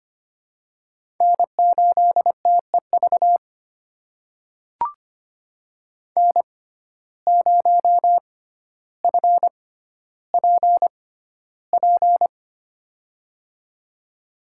Morse code, heard as a keyed tone, about 25 wpm, spaced as for 13 wpm, 700 Hz. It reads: N8TEV N0FPP
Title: USA Call Signs - 6x Character Spacing - 25wpm